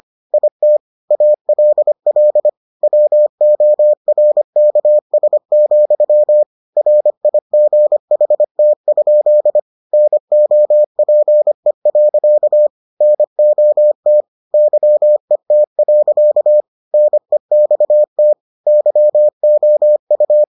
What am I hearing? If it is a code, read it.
Morse code, heard as a keyed tone, about 25 wpm, 600 Hz. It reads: IT ALL WORKS, RIGHT? NOPE. NOT YET. NEXT YOU